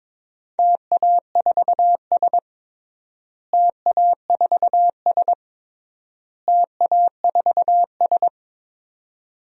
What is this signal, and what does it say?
Morse code, heard as a keyed tone, about 22 wpm, 700 Hz. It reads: TA4S TA4S TA4S
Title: DXCC-100 Call Signs - 4 Characters - Repeated 3x - 22wpm